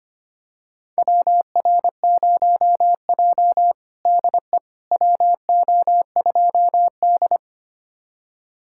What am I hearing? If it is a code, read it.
Morse code, heard as a keyed tone, about 25 wpm, 700 Hz. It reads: WR0J DE WO2D
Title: USA Callsign DE Callsign - 4 Characters - 25wpm